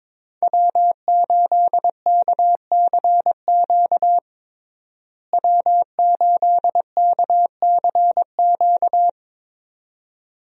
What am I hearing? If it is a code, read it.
Morse code, heard as a keyed tone, about 22 wpm, 700 Hz. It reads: W8KCQ W8KCQ
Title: USA Call Signs - 5 Characters - Repeated 2x - 22wpm